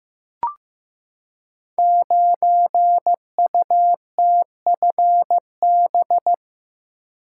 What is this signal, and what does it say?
Morse code, heard as a keyed tone, about 15 wpm, 700 Hz. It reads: 9UTFB
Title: Code Groups - 5 Letter:Numbers - 15wpm